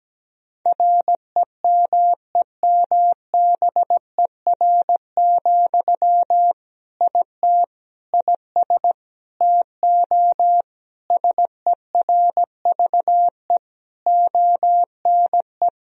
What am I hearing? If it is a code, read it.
Morse code, heard as a keyed tone, about 17 wpm, 700 Hz. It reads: REMEMBER, IT IS TO SERVE ONE
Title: Sentences from Top 1000 Words - Review Part 2 - 17wpm